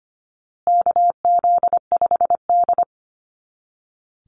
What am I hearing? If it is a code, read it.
Morse code, heard as a keyed tone, about 25 wpm, 700 Hz. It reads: KZ5D
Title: Call Signs Repeated 2x - 25wpm